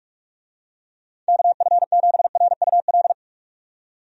Morse code, heard as a keyed tone, about 48 wpm, 700 Hz. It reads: KF7RUL
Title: USA Call Signs - 6 Characters - 45wpm